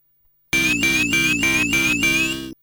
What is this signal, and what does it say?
There is no code.